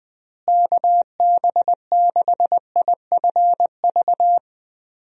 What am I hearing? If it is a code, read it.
Morse code, heard as a keyed tone, about 20 wpm, 700 Hz. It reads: KB6IFV
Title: USA Call Signs - 6 Characters - Repeated 3x - 20wpm